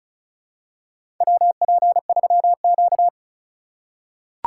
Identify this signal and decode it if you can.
Morse code, heard as a keyed tone, about 36 wpm, 700 Hz. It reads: WP3Q
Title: USA Call Signs - 4 Characters - 2x Character Spacing - 35wpm